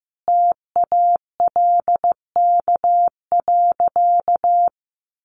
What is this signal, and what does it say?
Morse code, heard as a keyed tone, about 15 wpm, 700 Hz. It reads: TALK.
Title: Sentences from Top 1400 Words 15wpm